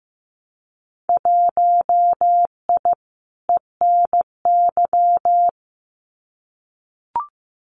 Morse code, heard as a keyed tone, about 15 wpm, 700 Hz. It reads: 1I ENY
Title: Winter Field Day Exchanges Encore 15wpm